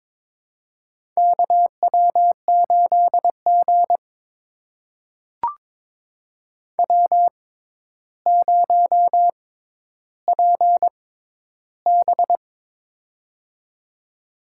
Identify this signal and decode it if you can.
Morse code, heard as a keyed tone, about 21 wpm, spaced as for 11 wpm, 700 Hz. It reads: KW8G W0PB
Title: USA Call Signs - 4 Characters - 6x Character Spacing - 22wpm